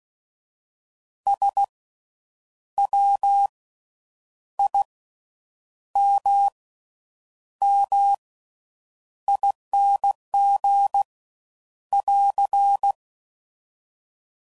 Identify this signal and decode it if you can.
Morse code, heard as a keyed tone, about 16 wpm, spaced as for 8 wpm, 800 Hz. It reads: SWIMMING <AR>
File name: Suffixes / Special INGs / Slow With Faster Ings